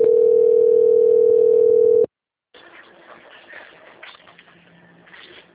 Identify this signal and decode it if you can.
Ringback tone (North America), 440 + 480 Hz.